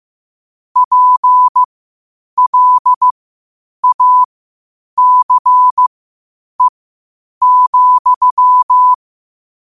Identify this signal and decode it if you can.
Morse code, heard as a keyed tone, about 15 wpm, spaced as for 8 wpm, 1000 Hz. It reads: PLACE,
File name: qotdfiles / 8wpm